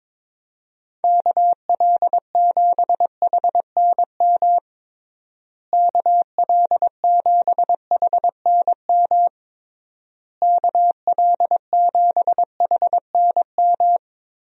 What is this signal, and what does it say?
Morse code, heard as a keyed tone, about 22 wpm, 700 Hz. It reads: KL7HNM KL7HNM KL7HNM
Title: USA Call Signs - 6 Characters - Repeated 3x - 22wpm